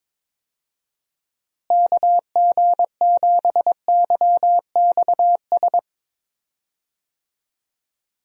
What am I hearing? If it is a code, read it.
Morse code, heard as a keyed tone, about 22 wpm, 700 Hz. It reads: KG7YXS
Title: USA Call Signs - 6 Characters - Repeated 3x - 22wpm